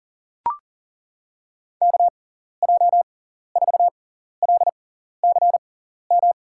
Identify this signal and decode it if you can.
Morse code, heard as a keyed tone, about 37 wpm, spaced as for 14 wpm, 700 Hz. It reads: KJ4LCM
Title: USA Call Signs - 6 Characters - 6x Character Spacing - 40wpm